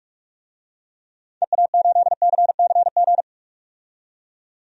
Morse code, heard as a keyed tone, about 42 wpm, 700 Hz. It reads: EA8CCG
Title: DXCC-100 Call Signs - 6 Characters - 45wpm